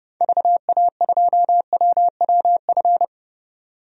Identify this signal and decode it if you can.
Morse code, heard as a keyed tone, about 30 wpm, 700 Hz. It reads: VA2WWF
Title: DXCC-100 Call Signs - 6 Characters - 30wpm